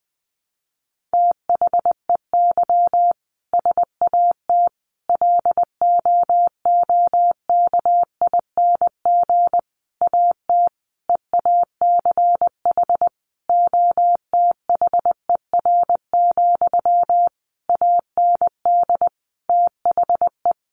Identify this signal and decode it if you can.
Morse code, heard as a keyed tone, about 20 wpm, 700 Hz. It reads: THEY SAT LOOKING AT EACH OTHER, AND THE
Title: A Sound Of Thunder 20wpm